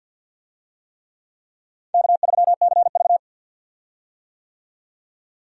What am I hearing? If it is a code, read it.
Morse code, heard as a keyed tone, about 48 wpm, 700 Hz. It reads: K3CV
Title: USA Call Signs - 4 Characters - 50wpm